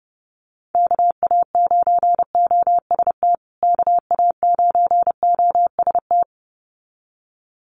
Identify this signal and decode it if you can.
Morse code, heard as a keyed tone, about 30 wpm, 700 Hz. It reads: KA9OST KA9OST
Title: USA Call Signs Repeated 2x 30wpm